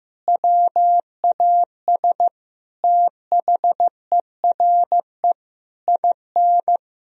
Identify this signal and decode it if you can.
Morse code, heard as a keyed tone, about 15 wpm, 700 Hz. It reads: WAS THERE IN
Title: Sentences from Top 1000 Words - Review Part 3 - 15wpm